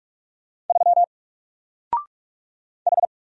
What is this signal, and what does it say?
Morse code, heard as a keyed tone, about 44 wpm, 700 Hz. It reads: 3 H
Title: Single Letter-Number Alt 45wpm